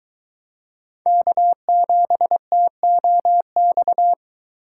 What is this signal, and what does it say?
Morse code, heard as a keyed tone, about 23 wpm, 700 Hz. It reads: K7TOX